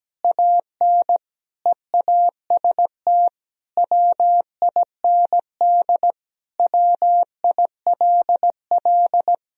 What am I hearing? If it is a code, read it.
Morse code, heard as a keyed tone, about 17 wpm, 700 Hz. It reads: AN EAST WIND WILL
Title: Sentences from Top 1000 Words - Review Part 3 - 17wpm